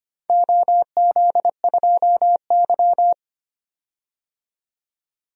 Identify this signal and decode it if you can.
Morse code, heard as a keyed tone, about 25 wpm, 700 Hz. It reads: OZ2Y
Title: DXCC-100 Call Signs - 4 Characters - 25wpm